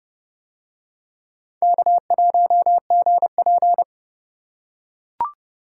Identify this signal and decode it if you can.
Morse code, heard as a keyed tone, about 26 wpm, 700 Hz. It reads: K1GP